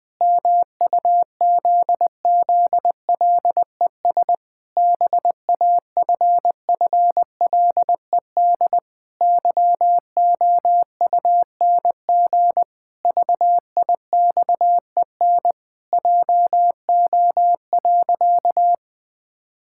Morse code, heard as a keyed tone, about 21 wpm, 700 Hz. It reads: MUZZLES BAFFLED YOUNG VIXEN JO.
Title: Pangram Challenge v2 - 20wpm